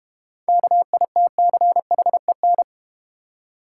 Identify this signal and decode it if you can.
Morse code, heard as a keyed tone, about 31 wpm, 690 Hz. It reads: KITCHEN